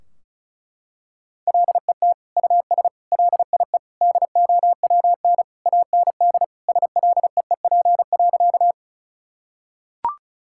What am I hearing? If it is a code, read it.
Morse code, heard as a keyed tone, about 36 wpm, 700 Hz. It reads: LET US LIE DOWN AND SLEEP.
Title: Sentences from Top 1600 Words 35wpm